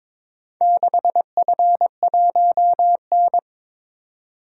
Morse code, heard as a keyed tone, about 23 wpm, 700 Hz. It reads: 6F1N
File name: practice / 22wpm / Four Letter-Number Combinations - 22wpm